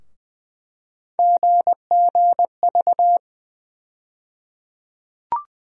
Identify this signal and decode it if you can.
Morse code, heard as a keyed tone, about 21 wpm, 700 Hz. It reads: GGV
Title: USA Call Sign Postfixes - 3 Letters 20wpm